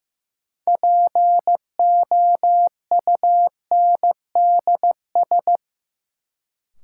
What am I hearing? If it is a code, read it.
Morse code, heard as a keyed tone, about 15 wpm, 700 Hz. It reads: POUNDS
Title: Sentences from Top 1000 Words - Review Part 2 - New Word Drill - 15wpm